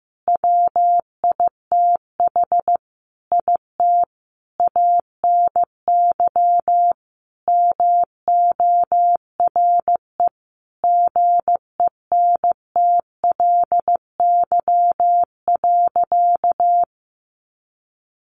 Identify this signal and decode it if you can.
Morse code, heard as a keyed tone, about 15 wpm, 700 Hz. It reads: WITH IT ANY MORE GENTLY.
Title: Sentences from Top 1500 Words 15wpm